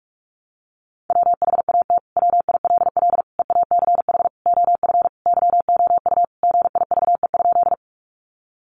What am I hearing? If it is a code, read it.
Morse code, heard as a keyed tone, about 44 wpm, 700 Hz. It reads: WHAT WILL EACH OF YOU GIVE?
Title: Sentences from Top 200 Words 45wpm